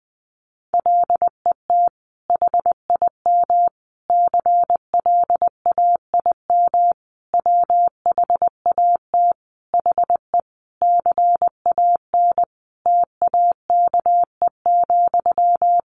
Morse code, heard as a keyed tone, about 19 wpm, 700 Hz. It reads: LET HIM CLAIM WHAT HE CAN TAKE,
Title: Sentences from Top 1500 Words 20wpm